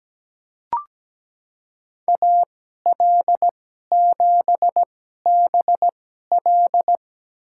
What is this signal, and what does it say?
Morse code, heard as a keyed tone, about 17 wpm, spaced as for 12 wpm, 700 Hz. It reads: AL7BL
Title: USA Call Signs - 5 Characters - 2x Character Spacing - 17wpm